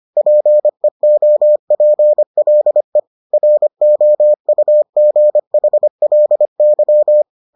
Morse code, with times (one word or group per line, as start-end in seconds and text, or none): 0.2-3.0 PEOPLE
3.3-7.2 ROUGHLY